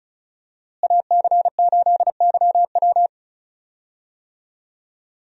Morse code, heard as a keyed tone, about 36 wpm, 700 Hz. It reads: AC8YW
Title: USA Call Signs - 5 Characters - Repeated 2x - 35wpm